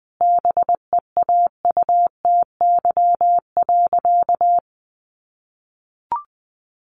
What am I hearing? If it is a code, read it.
Morse code, heard as a keyed tone, about 20 wpm, 700 Hz. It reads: BEAUTY.
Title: Sentences from Top 1400 Words 20wpm